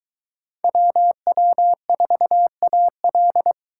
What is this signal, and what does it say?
Morse code, heard as a keyed tone, about 21 wpm, 700 Hz. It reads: WW4AL